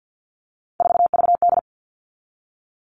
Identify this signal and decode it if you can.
Morse code, heard as a keyed tone, about 50 wpm, 700 Hz. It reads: 4VD
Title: All Cut Numbers - 3-Digit 50wpm